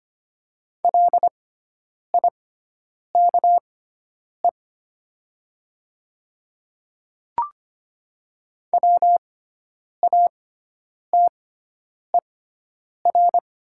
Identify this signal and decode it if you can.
Morse code, heard as a keyed tone, about 25 wpm, spaced as for 9 wpm, 700 Hz. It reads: LIKE WATER
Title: 6x Character Spacing - Top 500 Words - 25wpm